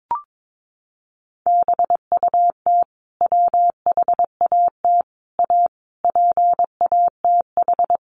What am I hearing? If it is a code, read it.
Morse code, heard as a keyed tone, about 22 wpm, 700 Hz. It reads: BUT WHAT A PATH
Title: Sentences from Top 1400 Words 22wpm